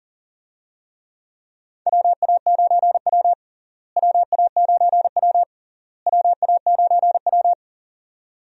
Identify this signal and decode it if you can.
Morse code, heard as a keyed tone, about 37 wpm, 700 Hz. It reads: WA9W WA9W WA9W
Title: USA Call Signs - 4 Characters - Repeated 3x - 40wpm